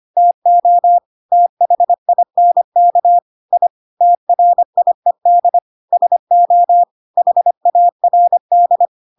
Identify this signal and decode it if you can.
Morse code, heard as a keyed tone, about 25 wpm, 700 Hz. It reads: TO THINK I TRIED SO HARD
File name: Pinocchio / Pinocchio ch3 25 25 ik1hge